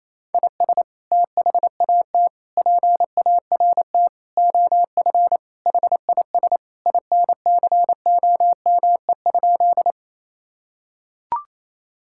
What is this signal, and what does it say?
Morse code, heard as a keyed tone, about 27 wpm, 700 Hz. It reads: IS THAT PART OF HIS INCOME?
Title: Sentences from Top 1000 Words - Review Part 5 - 28wpm